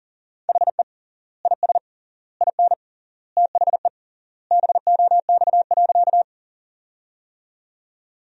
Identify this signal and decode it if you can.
Morse code, heard as a keyed tone, about 40 wpm, 700 Hz. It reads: HE IS IN THE BOX.
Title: 3x Word-Spacing - Sentences from Top 500 Words - 40wpm